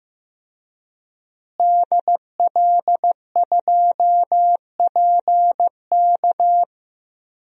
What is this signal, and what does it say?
Morse code, heard as a keyed tone, about 15 wpm, 700 Hz. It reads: DL2PK
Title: DXCC-100 Call Signs - 5 Characters - Repeated 3x - 15wpm